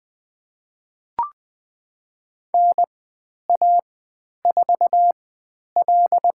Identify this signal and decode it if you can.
Morse code, heard as a keyed tone, about 20 wpm, spaced as for 10 wpm, 700 Hz. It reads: NA4L
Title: 095 Lesson 16g - Letter c - Call Signs 20wpm